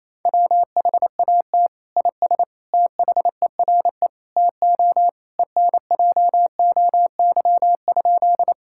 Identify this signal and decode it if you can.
Morse code, heard as a keyed tone, about 28 wpm, 700 Hz. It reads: WHAT IS THERE TO ENJOY?
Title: Sentences from Top 1000 Words - Review Part 5 - 28wpm